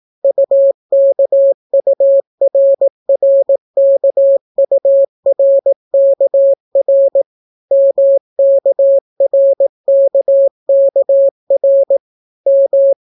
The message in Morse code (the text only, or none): UKURRKURKR MKRKKR M